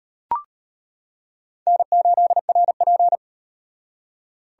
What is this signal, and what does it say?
Morse code, heard as a keyed tone, about 33 wpm, 700 Hz. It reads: N8RP